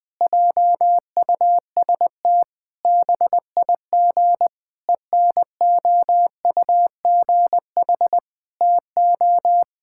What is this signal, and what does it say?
Morse code, heard as a keyed tone, about 20 wpm, 700 Hz. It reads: JUST BIG ENOUGH TO
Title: Sentences from Top 1000 Words - Review Part 5 - 20wpm